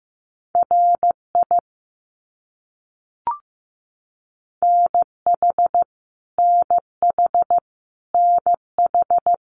Morse code, heard as a keyed tone, about 15 wpm, 700 Hz. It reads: RI NH NH NH
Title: US States Repeated 3x 15wpm